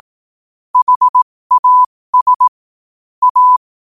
Morse code, heard as a keyed tone, about 18 wpm, spaced as for 15 wpm, 1000 Hz. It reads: HAS A
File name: qotdfiles / 15wpm